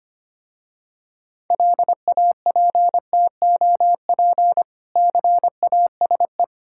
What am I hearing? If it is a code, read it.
Morse code, heard as a keyed tone, about 25 wpm, 700 Hz. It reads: LAPTOP CASE
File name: practice / 25wpm / Difficult Words with Hints - 25wpm